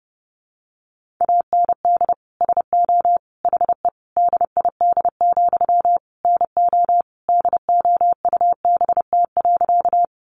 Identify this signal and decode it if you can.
Morse code, heard as a keyed tone, about 30 wpm, 700 Hz. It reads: AND SO HE DID, NO DOUBT.
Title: Sentences from Top 1500 Words 30wpm